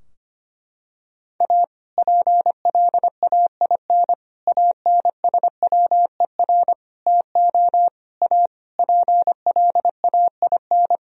Morse code, heard as a keyed tone, about 25 wpm, 700 Hz. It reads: A PLAIN ANSWER TO A PLAIN
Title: Sentences from Top 1000 Words - Review Part 2 - 25wpm